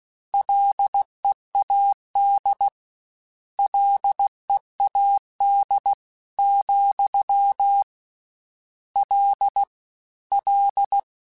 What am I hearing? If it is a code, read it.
Morse code, heard as a keyed tone, about 16 wpm, 800 Hz. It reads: LEAD LEAD, L L
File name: Combinations / Combinations-mp3s / Combinations@15